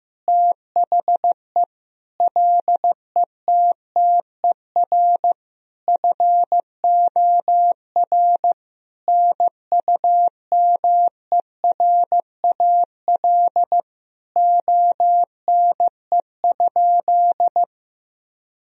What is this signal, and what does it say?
Morse code, heard as a keyed tone, about 15 wpm, 700 Hz. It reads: THE LETTER FOR NUMERAL ONE?
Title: Sentences from Top 1000 Words - Review Part 2 - 15wpm